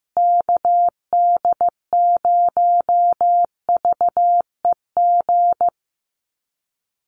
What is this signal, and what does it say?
Morse code, heard as a keyed tone, about 15 wpm, 700 Hz. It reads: KD0VEG
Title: USA Call Signs Repeated 2x 15wpm